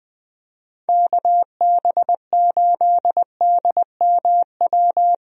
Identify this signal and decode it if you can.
Morse code, heard as a keyed tone, about 20 wpm, 700 Hz. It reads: KB8DMW
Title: USA Call Signs - 6 Characters - Repeated 2x - 20wpm